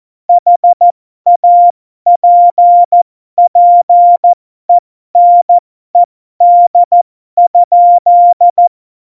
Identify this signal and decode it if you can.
Morse code, heard as a keyed tone, about 14 wpm, spaced as for 12 wpm, 700 Hz. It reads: HAPPENED?
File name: Morsekurs / Witze